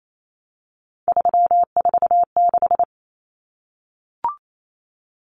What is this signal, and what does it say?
Morse code, heard as a keyed tone, about 28 wpm, 700 Hz. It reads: 346
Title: Common Cut Numbers - 3-Digit 28wpm